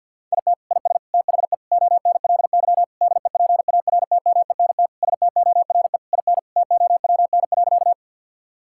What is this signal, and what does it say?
Morse code, heard as a keyed tone, about 50 wpm, 700 Hz. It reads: IT IS THE ONLY DEPARTMENT STORE IN TOWN.
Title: Sentences from Top 1000 Words - Review Part 5 - 50wpm